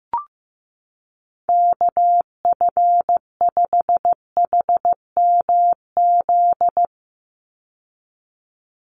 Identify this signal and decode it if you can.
Morse code, heard as a keyed tone, about 15 wpm, 700 Hz. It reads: KF5HMZ
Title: USA Call Signs 15wpm